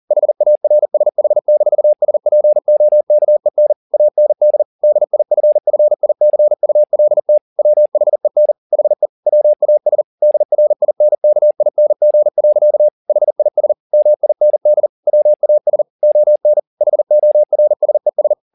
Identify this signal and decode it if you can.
Morse code, heard as a keyed tone, about 40 wpm, 600 Hz. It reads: HARSH-SPOKEN AND DIFFICULT WHEN HE WAS DRINKING. HIS MIND WAS ON HORSES